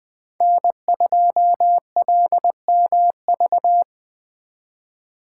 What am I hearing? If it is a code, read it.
Morse code, heard as a keyed tone, about 24 wpm, 700 Hz. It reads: N2LMV
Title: Speed Racing - USA 1x3 Call Signs 20wpm